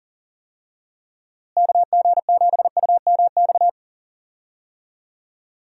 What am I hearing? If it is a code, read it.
Morse code, heard as a keyed tone, about 40 wpm, 700 Hz. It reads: KG7UMX
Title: USA Call Signs - 6 Characters - Repeated 3x - 40wpm